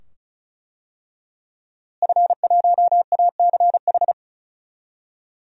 Morse code, heard as a keyed tone, about 36 wpm, 700 Hz. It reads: F1ACH
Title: DXCC-100 Call Signs - 5 Characters - Repeated 2x - 35wpm